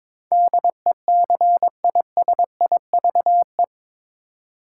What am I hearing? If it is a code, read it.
Morse code, heard as a keyed tone, about 22 wpm, 700 Hz. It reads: DECISIVE
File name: practice / 22wpm / Eight Letter Words - 22wpm